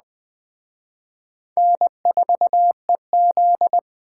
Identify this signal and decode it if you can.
Morse code, heard as a keyed tone, about 20 wpm, 700 Hz. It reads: N4EZ